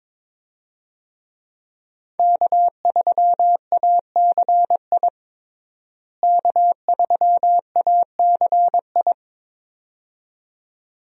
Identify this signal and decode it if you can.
Morse code, heard as a keyed tone, about 22 wpm, 700 Hz. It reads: K3ACI K3ACI
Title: USA Call Signs - 5 Characters - Repeated 2x - 22wpm